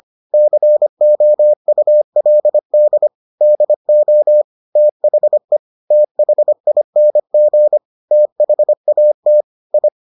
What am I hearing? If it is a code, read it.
Morse code, heard as a keyed tone, about 25 wpm, 600 Hz. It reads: COULD DO THE THING THAT I